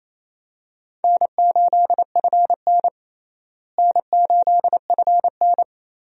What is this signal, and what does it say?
Morse code, heard as a keyed tone, about 28 wpm, 700 Hz. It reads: N8FN N8FN
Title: USA Call Signs - 4 Characters - Repeated 2x - 28wpm